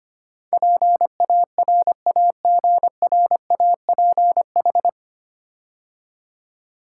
Morse code, heard as a keyed tone, about 25 wpm, 700 Hz. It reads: PARAGRAPH
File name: practice / 25wpm / 9 Letter Words - First 3 Letters Given - 25wpm